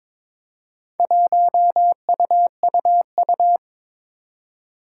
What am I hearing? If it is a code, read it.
Morse code, heard as a keyed tone, about 22 wpm, 700 Hz. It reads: 1UUU
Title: Four Challenging Letter-Number Combinations 22wpm